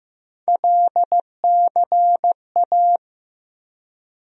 Morse code, heard as a keyed tone, about 16 wpm, 700 Hz. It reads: LCA
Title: USA Call Sign Postfixes - 3 Letters 15wpm